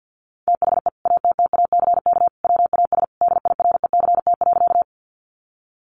Morse code, heard as a keyed tone, about 50 wpm, 700 Hz. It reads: THE ATTACK WAS DIRECT.
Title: Sentences from Top 1500 Words 50wpm